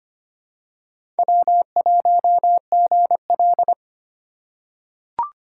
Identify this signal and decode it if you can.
Morse code, heard as a keyed tone, about 25 wpm, 700 Hz. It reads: W1GL
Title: USA Call Signs - 4 Characters - 4x Character Spacing - 25wpm